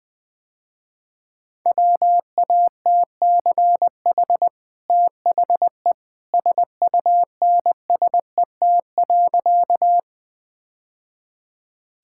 Morse code, heard as a keyed tone, about 21 wpm, 700 Hz. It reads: WATCH THE SUNSET.
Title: Easy Sentences with Hints - 20wpm